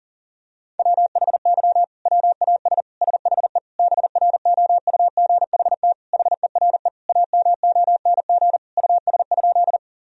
Morse code, heard as a keyed tone, about 40 wpm, 700 Hz. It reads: WHY WAS SHE BROUGHT HERE AMONG US?
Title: Sentences from Top 1000 Words - Review Part 3 - 40wpm